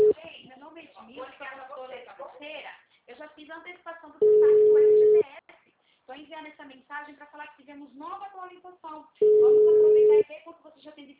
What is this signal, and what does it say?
Ringback tone (Europe, 425 Hz), with speech.
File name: out- -2002- - - .22423